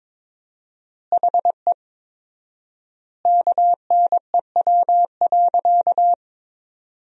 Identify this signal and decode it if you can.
Morse code, heard as a keyed tone, about 22 wpm, 700 Hz. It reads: HE KNEW.
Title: 4x Word-Spacing - Sentences from Top 400 Words - 22wpm